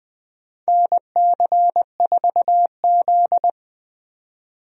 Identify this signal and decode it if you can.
Morse code, heard as a keyed tone, about 20 wpm, 700 Hz. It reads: NC4Z
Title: USA Call Signs - 4 Characters - 20wpm